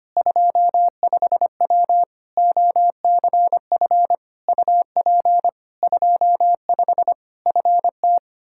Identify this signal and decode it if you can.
Morse code, heard as a keyed tone, about 25 wpm, 700 Hz. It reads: 25W OCF UP 25 FT
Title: Practice QSO - Exchange 2 - Info - 25wpm